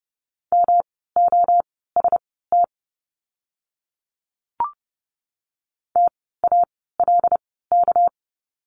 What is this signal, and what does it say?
Morse code, heard as a keyed tone, about 30 wpm, spaced as for 17 wpm, 700 Hz. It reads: MOST TALK
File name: course / 30:10wpm / 237 Lesson 38f - Letter q - Words 30wpm